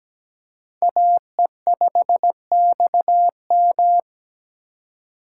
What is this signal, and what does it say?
Morse code, heard as a keyed tone, about 17 wpm, 700 Hz. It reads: AE5XM
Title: USA Call Signs - 5 Characters - 3x Character Spacing - 17wpm